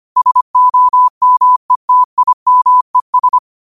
Morse code, heard as a keyed tone, about 25 wpm, 1000 Hz. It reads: SOMETIMES
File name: qotdfiles / 25wpm